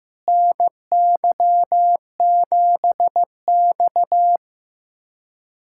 Morse code, heard as a keyed tone, about 15 wpm, 700 Hz. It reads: NY7X
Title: USA Callsign DE Callsign - 4 Characters - 15wpm